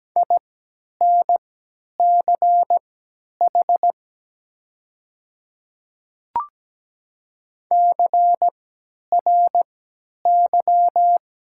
Morse code, heard as a keyed tone, about 17 wpm, spaced as for 10 wpm, 700 Hz. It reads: INCH CRY
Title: 3x Character Spacing - Top 500 Words - 17wpm